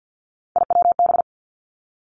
Morse code, heard as a keyed tone, about 50 wpm, 700 Hz. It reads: IWB
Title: Three Challenging Letter-Number Combinations 50wpm